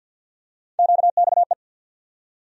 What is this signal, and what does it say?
Morse code, heard as a keyed tone, about 47 wpm, 700 Hz. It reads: QXE
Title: USA Call Sign Postfixes - 3 Letters 50wpm